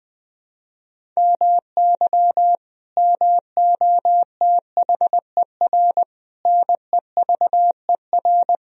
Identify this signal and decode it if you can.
Morse code, heard as a keyed tone, about 21 wpm, 700 Hz. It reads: MY MOTHER NEVER
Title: Sentences from Top 1000 Words - Review Part 1 - 20wpm